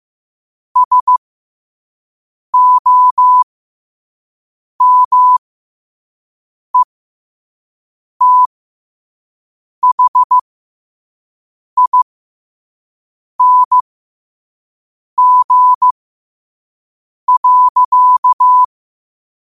Morse code, heard as a keyed tone, about 15 wpm, spaced as for 5 wpm, 1000 Hz. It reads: SOMETHING.
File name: qotdfiles / 5wpm